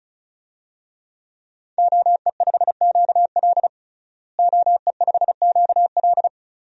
Morse code, heard as a keyed tone, about 36 wpm, 700 Hz. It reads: OE5QL OE5QL
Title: DXCC-100 Call Signs - 5 Characters - Repeated 2x - 35wpm